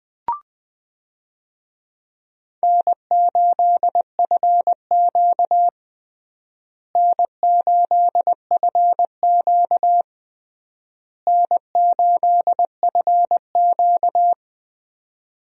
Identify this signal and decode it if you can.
Morse code, heard as a keyed tone, about 20 wpm, 700 Hz. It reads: N8FQ N8FQ N8FQ
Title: USA Call Signs - 4 Characters - Repeated 3x - 20wpm